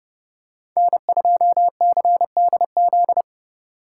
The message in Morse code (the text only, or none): N2CDZ